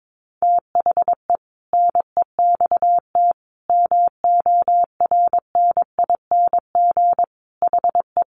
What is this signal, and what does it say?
Morse code, heard as a keyed tone, about 22 wpm, 700 Hz. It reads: THE NEXT MORNING HE
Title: Sentences from Top 600 Words 22wpm